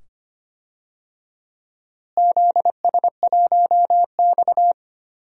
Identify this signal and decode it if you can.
Morse code, heard as a keyed tone, about 25 wpm, 700 Hz. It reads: ZS1X
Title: DXCC-100 Call Signs - 4 Characters - Repeated 2x - 25wpm